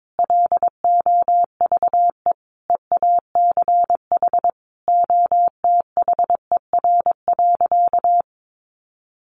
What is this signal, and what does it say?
Morse code, heard as a keyed tone, about 22 wpm, 700 Hz. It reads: LOVE EACH OTHER.
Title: Sentences from Top 1500 Words 22wpm